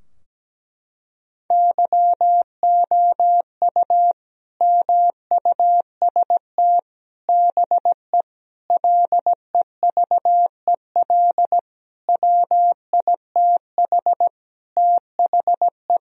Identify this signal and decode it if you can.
Morse code, heard as a keyed tone, about 17 wpm, 700 Hz. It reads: YOU MUST BE LEVEL WITH THE
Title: Sentences from Top 1000 Words - Review Part 4 - 17wpm